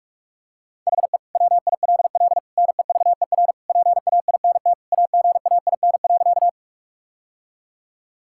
Morse code, heard as a keyed tone, about 46 wpm, 700 Hz. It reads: HE WILL NEVER PAINT AGAIN.
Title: Sentences from Top 1000 Words - Review Part 6 - 45wpm